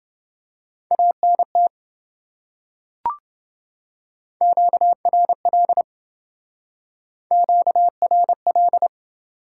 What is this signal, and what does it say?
Morse code, heard as a keyed tone, about 28 wpm, 700 Hz. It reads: ANT QRL QRL